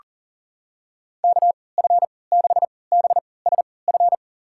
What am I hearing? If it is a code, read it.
Morse code, heard as a keyed tone, about 40 wpm, spaced as for 23 wpm, 700 Hz. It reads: KF6BSF
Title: USA Call Signs - 3x Character Spacing - 40wpm